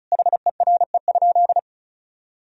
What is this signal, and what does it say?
Morse code, heard as a keyed tone, about 36 wpm, 700 Hz. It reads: HERE?
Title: Sentences from Top 1000 Words - Review Part 4 - 35wpm